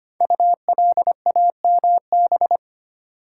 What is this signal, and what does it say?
Morse code, heard as a keyed tone, about 25 wpm, 690 Hz. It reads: ULAMB
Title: Code Groups - 5 Letters - 25wpm